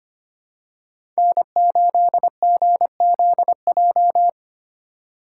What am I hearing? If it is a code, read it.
Morse code, heard as a keyed tone, about 25 wpm, 700 Hz. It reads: N8GZJ
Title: USA Call Signs - 5 Characters - 3x Character Spacing - 25wpm